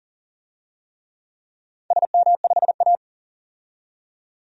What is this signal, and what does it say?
Morse code, heard as a keyed tone, about 42 wpm, 700 Hz. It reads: SM5A
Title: DXCC-100 Call Signs - 4 Characters - 40wpm